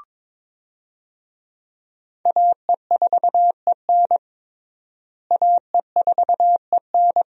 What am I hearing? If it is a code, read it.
Morse code, heard as a keyed tone, about 22 wpm, 700 Hz. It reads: AE4EN AE4EN
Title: USA Call Signs - 5 Characters - Repeated 3x - 22wpm